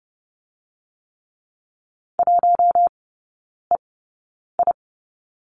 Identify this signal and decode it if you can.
Morse code, heard as a keyed tone, about 30 wpm, spaced as for 9 wpm, 700 Hz. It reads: 1EI
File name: course / 30:10wpm / 087 Lesson 15e - Number 5 - Three Characters 30wpm